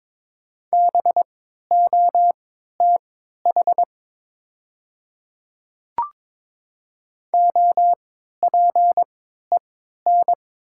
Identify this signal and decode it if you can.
Morse code, heard as a keyed tone, about 22 wpm, spaced as for 12 wpm, 700 Hz. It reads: BOTH OPEN
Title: 3x Character Spacing - Top 500 Words - 22wpm